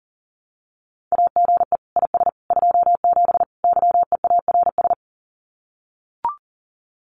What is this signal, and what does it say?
Morse code, heard as a keyed tone, about 37 wpm, 700 Hz. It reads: AGE IS 27 YEARS
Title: Common QSO Phrases Encore 40wpm